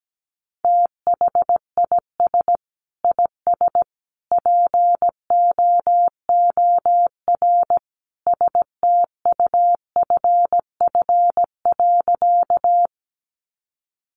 Morse code, heard as a keyed tone, about 17 wpm, 700 Hz. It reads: THIS IS POOR STUFF.
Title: Sentences from Top 1400 Words 17wpm